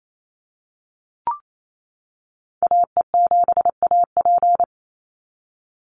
Morse code, heard as a keyed tone, about 28 wpm, 700 Hz. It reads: AE7AP
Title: NA SOTA Activator Callsigns 28wpm